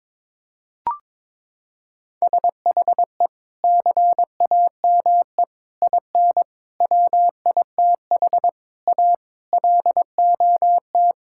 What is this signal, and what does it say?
Morse code, heard as a keyed tone, about 22 wpm, 700 Hz. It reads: SHE CAME IN WITH A LOT
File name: practice / 22wpm / Sentences with 4-Word Phrases - 22wpm